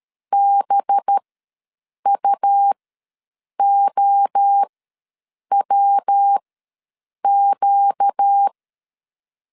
Morse code, heard as a keyed tone, about 13 wpm, spaced as for 7 wpm, 800 Hz. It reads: BUOWQ
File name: LEZIONI / 5 LBVC / 5 LB ws 12wpm 3034